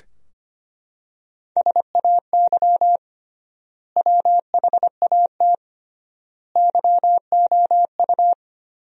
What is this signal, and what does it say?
Morse code, heard as a keyed tone, about 25 wpm, 700 Hz. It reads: SAY WHAT YOU
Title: 3x Word-Spacing - Sentences from Top 200 Words - 25wpm